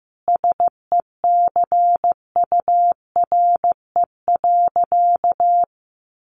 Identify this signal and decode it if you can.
Morse code, heard as a keyed tone, about 15 wpm, 700 Hz. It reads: SECURE.
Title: Sentences from Top 1200 Words 15wpm